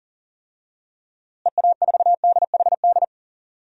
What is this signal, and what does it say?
Morse code, heard as a keyed tone, about 40 wpm, 700 Hz. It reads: EA4DHD
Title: DXCC-100 Call Signs - 6 Characters - Repeated 2x - 40wpm